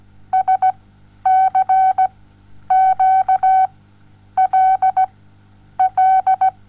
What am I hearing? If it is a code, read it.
Morse code, heard as a keyed tone, about 17 wpm, spaced as for 9 wpm, 760 Hz. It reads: SCQLL
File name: Onlinelehrgang / morsen / DJ4UF Morsen07